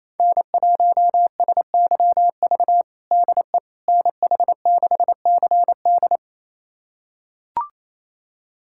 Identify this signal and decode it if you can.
Morse code, heard as a keyed tone, about 27 wpm, 700 Hz. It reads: N1SYV DE NH6CD
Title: USA Callsign DE Callsign - 5 Characters - 28wpm